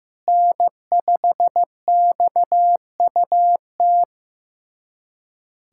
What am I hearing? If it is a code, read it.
Morse code, heard as a keyed tone, about 16 wpm, 700 Hz. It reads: N5XUT
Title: USA Callsign DE Callsign - 5 Characters - 15wpm